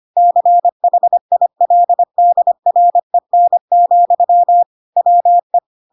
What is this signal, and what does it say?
Morse code, heard as a keyed tone, about 25 wpm, 700 Hz. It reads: CHILDREN, WE
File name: Pinocchio / Pinocchio ch7 25 25 ik1hge